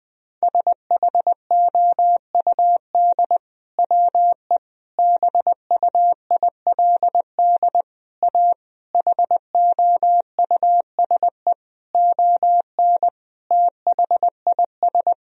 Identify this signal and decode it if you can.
Morse code, heard as a keyed tone, about 20 wpm, 700 Hz. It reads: SHOUD WE BUILD A HOUSE ON THIS
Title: Sentences from Top 1000 Words - Review Part 4 - 20wpm